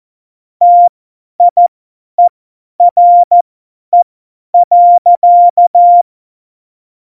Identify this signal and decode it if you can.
Morse code, heard as a keyed tone, about 14 wpm, spaced as for 10 wpm, 700 Hz. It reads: TIERE.
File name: Morsekurs / Witze / Witz 19. Januar 2021, 50 F70